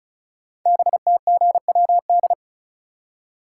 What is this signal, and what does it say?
Morse code, heard as a keyed tone, about 36 wpm, 690 Hz. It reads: BTGWD
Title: Code Groups - 5 Letters - 35wpm